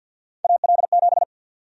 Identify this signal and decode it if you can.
Morse code, heard as a keyed tone, about 52 wpm, 700 Hz. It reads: AL7
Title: USA Call Sign Prefixes - 50wpm